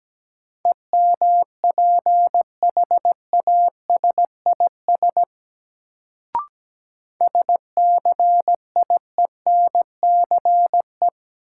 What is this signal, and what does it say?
Morse code, heard as a keyed tone, about 17 wpm, 700 Hz. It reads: EMPHASIS SCIENCE
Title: Sentences from Top 1400 Words - New Word Drill Alt - 17wpm